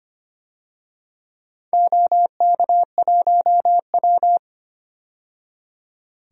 Morse code, heard as a keyed tone, about 25 wpm, 700 Hz. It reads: OK1W
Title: DXCC-100 Call Signs - 4 Characters - 25wpm